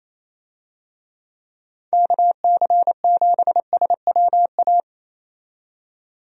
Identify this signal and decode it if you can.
Morse code, heard as a keyed tone, about 28 wpm, 700 Hz. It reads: KC7SWA
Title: USA Call Signs - 6 Characters - 28wpm